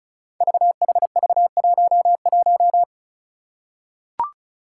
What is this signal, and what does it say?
Morse code, heard as a keyed tone, about 36 wpm, 700 Hz. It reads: VHV11
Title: Five Challenging Letter-Number Combinations 35wpm